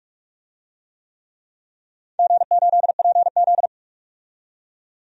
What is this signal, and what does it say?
Morse code, heard as a keyed tone, about 43 wpm, 700 Hz. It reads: G8PZ
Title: DXCC-100 Call Signs - 4 Characters - 45wpm